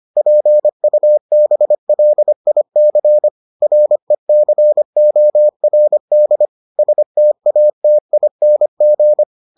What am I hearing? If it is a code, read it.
Morse code, heard as a keyed tone, about 25 wpm, 600 Hz. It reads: PUBLIC RECORD STATING